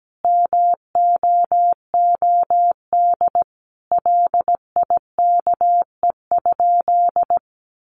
Morse code, heard as a keyed tone, about 17 wpm, 700 Hz. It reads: MOOD LIKE?
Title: Sentences from Top 1100 Words 17wpm